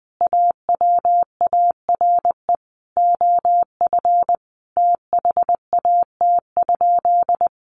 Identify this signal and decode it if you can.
Morse code, heard as a keyed tone, about 19 wpm, 700 Hz. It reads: AWARE OF THAT?
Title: Sentences from Top 1200 Words 20wpm